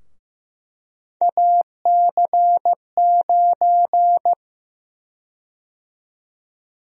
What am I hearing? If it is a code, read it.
Morse code, heard as a keyed tone, about 15 wpm, 700 Hz. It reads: AC9
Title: USA Call Sign Prefixes - 15wpm